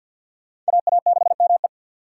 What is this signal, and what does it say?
Morse code, heard as a keyed tone, about 66 wpm, spaced as for 58 wpm, 700 Hz. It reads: AA6ME
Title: Speed Racing - USA 2x2 Call Signs 50wpm